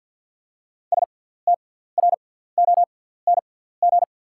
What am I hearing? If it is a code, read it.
Morse code, heard as a keyed tone, about 48 wpm, spaced as for 17 wpm, 690 Hz. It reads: STRONG
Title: 6x Character Spacing - Top 500 Words - 50wpm